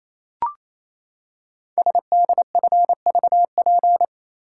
Morse code, heard as a keyed tone, about 27 wpm, 700 Hz. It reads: SDFVP
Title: Code Groups - 5 Letter:Numbers - 28wpm